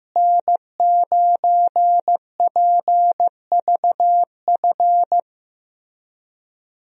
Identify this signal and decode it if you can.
Morse code, heard as a keyed tone, about 15 wpm, 700 Hz. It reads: N9PVF
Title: USA Call Signs - 5 Characters - Repeated 2x - 15wpm